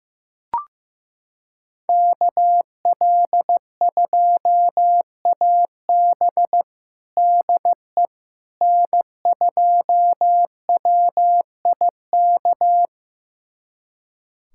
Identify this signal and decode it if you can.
Morse code, heard as a keyed tone, about 16 wpm, 700 Hz. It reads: KL2AB DE N2WIK
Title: USA Callsign DE Callsign - 5 Characters - 15wpm